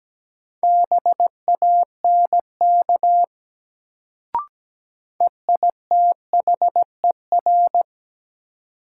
Morse code, heard as a keyed tone, about 17 wpm, 700 Hz. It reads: BANK EITHER
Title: Sets of 5 Words - New Word Drill - 17wpm